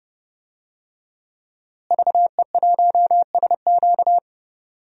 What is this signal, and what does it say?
Morse code, heard as a keyed tone, about 30 wpm, 700 Hz. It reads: VE1SQ
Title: DXCC-100 Call Signs - 5 Characters - Repeated 2x - 30wpm